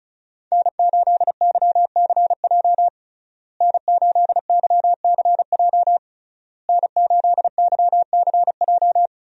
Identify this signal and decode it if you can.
Morse code, heard as a keyed tone, about 34 wpm, 700 Hz. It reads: N8YCJ N8YCJ N8YCJ
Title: USA Call Signs - 5 Characters - Repeated 3x - 35wpm